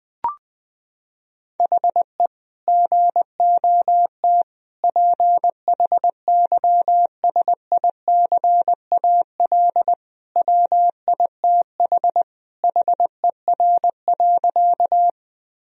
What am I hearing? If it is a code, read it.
Morse code, heard as a keyed tone, about 19 wpm, 700 Hz. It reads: HE GOT PHYSICAL WITH HER.
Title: Sentences from Top 1000 Words - Review Part 4 - 20wpm